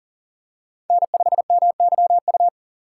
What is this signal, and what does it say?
Morse code, heard as a keyed tone, about 40 wpm, 700 Hz. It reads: N5MYU
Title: USA Call Signs - 5 Characters - 2x Character Spacing - 40wpm